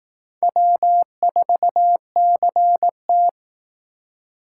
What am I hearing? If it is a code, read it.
Morse code, heard as a keyed tone, about 18 wpm, 700 Hz. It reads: W4CT
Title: Speed Racing - USA 1x2 Call Signs 15wpm